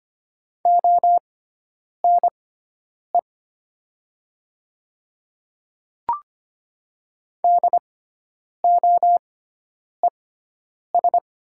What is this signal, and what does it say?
Morse code, heard as a keyed tone, about 25 wpm, spaced as for 9 wpm, 700 Hz. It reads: ONE DOES
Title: 6x Character Spacing - Top 300 Words - 25wpm